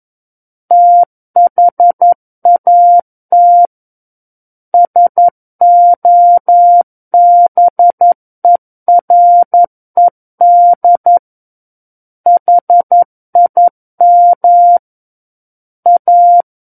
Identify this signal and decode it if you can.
Morse code, heard as a keyed tone, about 11 wpm, 700 Hz. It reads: THAT SOBERED HIM A
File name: Book One / 02-Chapter 2 - 11 WPM